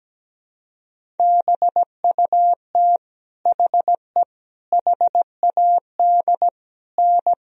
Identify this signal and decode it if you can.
Morse code, heard as a keyed tone, about 17 wpm, 700 Hz. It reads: BUT HE HAD N
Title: Sentences from Top 100 Words and Abbreviations 17wpm